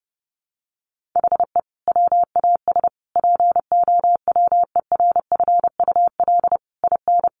Morse code, heard as a keyed tone, about 30 wpm, 700 Hz. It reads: HE WAS POWERFUL IN
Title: Sentences from Top 1300 Words 30wpm